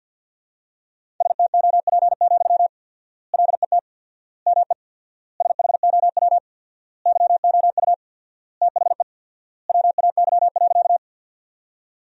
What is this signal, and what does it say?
Morse code, heard as a keyed tone, about 52 wpm, 700 Hz. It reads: STOP, LET ME SHOW YOU THE WAY.
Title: 4x Word-Spacing - Sentences from Top 300 Words - 50wpm